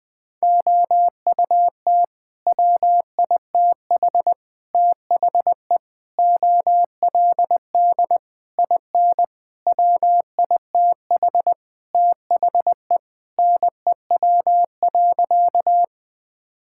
Morse code, heard as a keyed tone, about 20 wpm, 700 Hz. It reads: OUT WITH THE OLD IN WITH THE NEW.
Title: Sentences with 8-Word Phrases - 20wpm